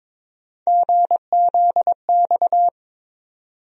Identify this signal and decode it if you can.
Morse code, heard as a keyed tone, about 22 wpm, 700 Hz. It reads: GZX
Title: USA Call Sign Postfixes - 3 Letters 22wpm